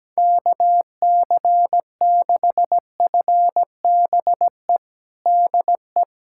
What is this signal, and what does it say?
Morse code, heard as a keyed tone, about 17 wpm, 700 Hz. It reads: KC6FBE DE
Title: USA Callsign DE Callsign - 6 Characters - 17wpm